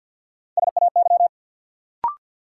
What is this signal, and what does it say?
Morse code, heard as a keyed tone, about 53 wpm, 700 Hz. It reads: SAY
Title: Sentences from Top 200 Words - New Word Drill - 50wpm